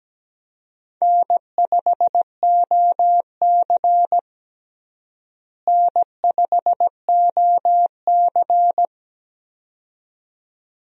Morse code, heard as a keyed tone, about 17 wpm, 700 Hz. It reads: N5OC N5OC
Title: USA Call Signs - 4 Characters - Repeated 2x - 17wpm